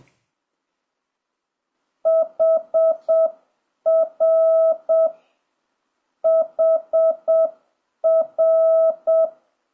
Morse code, heard as a keyed tone, about 7 wpm, 650 Hz. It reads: HR HR
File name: jh8cbh / 2026cw